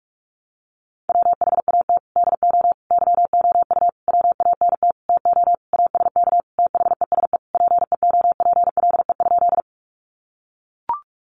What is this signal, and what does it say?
Morse code, heard as a keyed tone, about 44 wpm, 700 Hz. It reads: WHAT DO YOU WANT TO ASK THESE PEOPLE?
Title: Sentences from Top 200 Words 45wpm